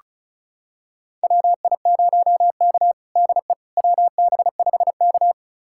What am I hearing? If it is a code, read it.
Morse code, heard as a keyed tone, about 34 wpm, 700 Hz. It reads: WI0K DE WB5K
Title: USA Callsign DE Callsign - 4 Characters - 35wpm